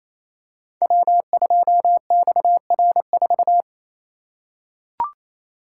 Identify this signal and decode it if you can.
Morse code, heard as a keyed tone, about 29 wpm, 700 Hz. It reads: W2XR4
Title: Code Groups - 5 Letter:Numbers - 28wpm